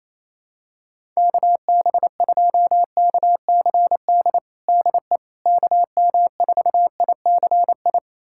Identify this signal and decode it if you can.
Morse code, heard as a keyed tone, about 28 wpm, 700 Hz. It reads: KB2KCD DE KM4ICI
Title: USA Callsign DE Callsign 28wpm